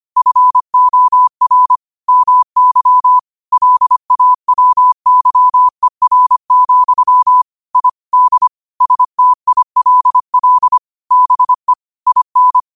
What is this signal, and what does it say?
Morse code, heard as a keyed tone, about 25 wpm, 1000 Hz. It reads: FOR MY LAWYER, I D STILL BE IN
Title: 25 WPM morse code quotes for Thu, 29 Feb 2024